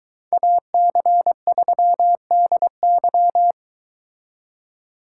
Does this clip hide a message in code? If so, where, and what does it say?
Yes: Morse code from 0.3 to 3.5 s, heard as a keyed tone, about 22 wpm, 700 Hz. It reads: AC3DY